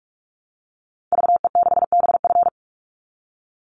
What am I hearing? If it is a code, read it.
Morse code, heard as a keyed tone, about 44 wpm, 700 Hz. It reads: VE6BF
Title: Call Signs Repeated 2x - 45wpm